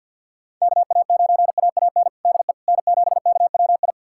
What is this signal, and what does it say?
Morse code, heard as a keyed tone, about 50 wpm, 690 Hz. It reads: KA9AAN DE N7KWI
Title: USA Callsign DE Callsign 50wpm